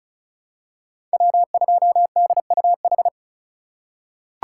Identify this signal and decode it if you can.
Morse code, heard as a keyed tone, about 36 wpm, 700 Hz. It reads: W2DUH
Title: USA Call Signs - 3x Character Spacing - 35wpm